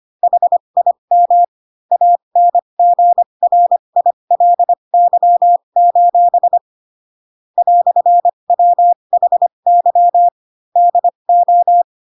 Morse code, heard as a keyed tone, about 25 wpm, spaced as for 22 wpm, 700 Hz. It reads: HIM ANGRILY: "WHY DO
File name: Pinocchio / Pinocchio ch2 18 25 ik1hge